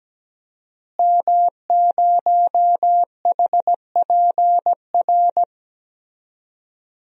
Morse code, heard as a keyed tone, about 17 wpm, 700 Hz. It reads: M0HPR
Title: DXCC-100 Call Signs - 5 Characters - Repeated 2x - 17wpm